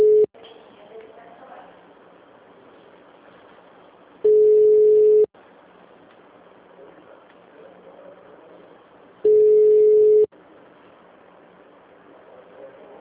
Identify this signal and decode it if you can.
Ringback tone (Europe, 425 Hz), with speech.